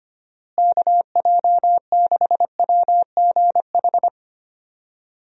Morse code, heard as a keyed tone, about 25 wpm, 700 Hz. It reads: KJ6WGH
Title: USA Call Signs - 6 Characters - Repeated 2x - 25wpm